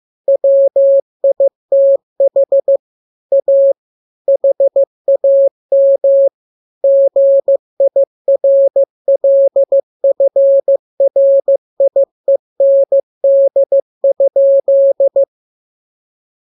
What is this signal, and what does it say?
Morse code, heard as a keyed tone, about 15 wpm, 550 Hz. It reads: WITH A HAM GIRLFRIEND?
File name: jokes / joke 18 15 0000